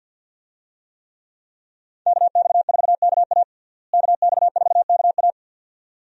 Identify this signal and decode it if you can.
Morse code, heard as a keyed tone, about 48 wpm, 700 Hz. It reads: KX4KA KX4KA
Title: USA Call Signs - 5 Characters - Repeated 2x - 50wpm